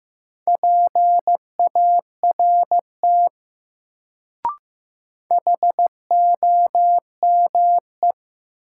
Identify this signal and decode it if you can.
Morse code, heard as a keyed tone, about 15 wpm, 700 Hz. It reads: PART HOME
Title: Sentences from Top 200 Words - New Word Drill - 15wpm